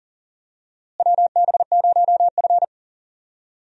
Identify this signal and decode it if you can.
Morse code, heard as a keyed tone, about 40 wpm, 700 Hz. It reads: WB0F
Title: USA Call Signs - 4 Characters - Repeated 2x - 40wpm